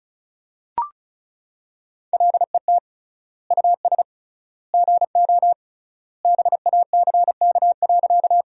Morse code, heard as a keyed tone, about 36 wpm, 700 Hz. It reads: LET US GO BACK.
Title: 3x Word-Spacing - Sentences from Top 300 Words - 35wpm